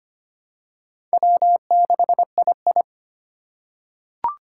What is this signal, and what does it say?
Morse code, heard as a keyed tone, about 25 wpm, 700 Hz. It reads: W6II
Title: USA Call Signs - 3x Character Spacing - 25wpm